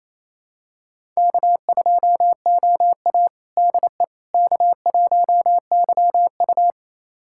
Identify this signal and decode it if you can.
Morse code, heard as a keyed tone, about 28 wpm, 700 Hz. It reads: K2OA DE K1YU
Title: USA Callsign DE Callsign - 4 Characters - 28wpm